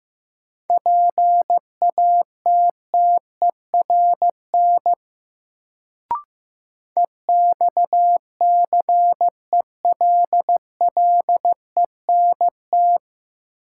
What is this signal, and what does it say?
Morse code, heard as a keyed tone, about 15 wpm, 700 Hz. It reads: PATTERN EXCELLENT
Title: Sentences from Top 1000 Words - New Word Drill - 15wpm